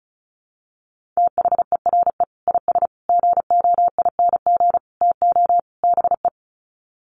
Morse code, heard as a keyed tone, about 34 wpm, 700 Hz. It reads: THERE IS GOING TO BE
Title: Sets of 5 Words Encore 35wpm